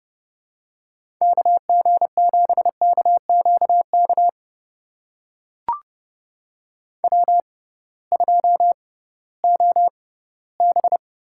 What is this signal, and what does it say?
Morse code, heard as a keyed tone, about 30 wpm, spaced as for 17 wpm, 700 Hz. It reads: KG7KQK W2OB
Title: USA Call Signs - 6x Character Spacing - 30wpm